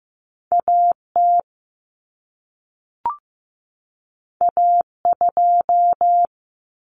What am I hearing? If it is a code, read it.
Morse code, heard as a keyed tone, about 15 wpm, 700 Hz. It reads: AT A2
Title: Common Cut Numbers - 2-Digit 15wpm